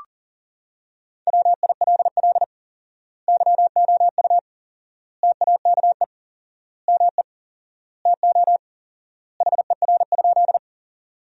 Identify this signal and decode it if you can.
Morse code, heard as a keyed tone, about 40 wpm, 700 Hz. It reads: WILL YOU TAKE ME TO HER?
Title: 4x Word-Spacing - Sentences from Top 200 Words - 40wpm